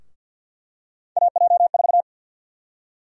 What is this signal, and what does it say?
Morse code, heard as a keyed tone, about 52 wpm, 700 Hz. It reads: AJ4
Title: USA Call Sign Prefixes - Repeated 2x - 50wpm